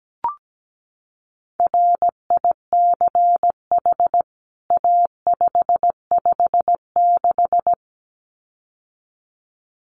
Morse code, heard as a keyed tone, about 17 wpm, 700 Hz. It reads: RICH A556
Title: CWT Exchange 17wpm